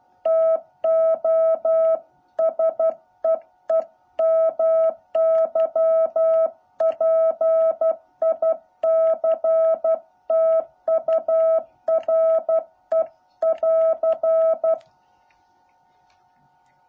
Morse code, heard as a keyed tone, about 12 wpm, 650 Hz. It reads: TO SEE MY PICTURE<AR>
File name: jh8cbh / 2021cw